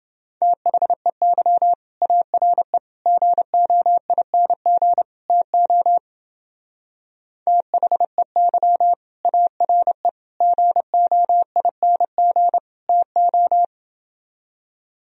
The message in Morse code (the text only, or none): THEY ARE GOING TO THEY ARE GOING TO